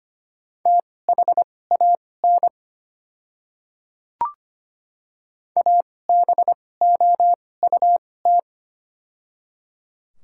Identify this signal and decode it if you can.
Morse code, heard as a keyed tone, about 25 wpm, spaced as for 18 wpm, 700 Hz. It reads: THAN ABOUT
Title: 2x Character Spacing - Top 100 Words - 25wpm